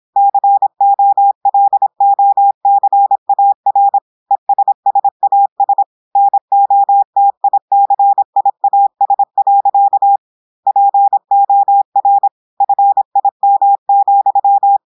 Morse code, heard as a keyed tone, about 26 wpm, 800 Hz. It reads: COLOCAR ESSAS NOTICIAS. POR FIM,
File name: portugais texte en clair 2 vitesse 26 mots minutes